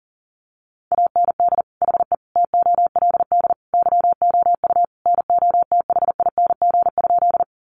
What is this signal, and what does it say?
Morse code, heard as a keyed tone, about 40 wpm, 700 Hz. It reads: AND HE TOLD YOU NOTHING?
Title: Sentences from Top 500 Words 40wpm